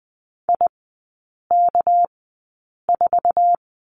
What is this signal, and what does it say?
Morse code, heard as a keyed tone, about 20 wpm, spaced as for 8 wpm, 700 Hz. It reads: IK4
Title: 198 Lesson 32e - Letter k - Three Characters 20wpm